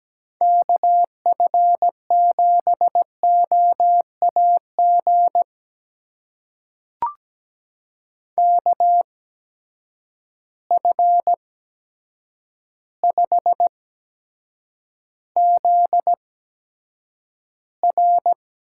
Morse code, heard as a keyed tone, about 17 wpm, spaced as for 8 wpm, 700 Hz. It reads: KF7OAG KF5ZR
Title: USA Call Signs - 6 Characters - 8x Character Spacing - 17wpm